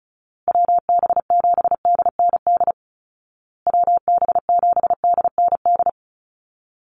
Morse code, heard as a keyed tone, about 34 wpm, 700 Hz. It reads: WB7DND WB7DND
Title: Call Signs Repeated 2x - 35wpm